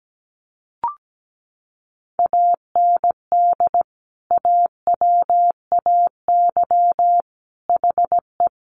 Morse code, heard as a keyed tone, about 17 wpm, 700 Hz. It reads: AND AWAY HE
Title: Sentences from Top 600 Words 17wpm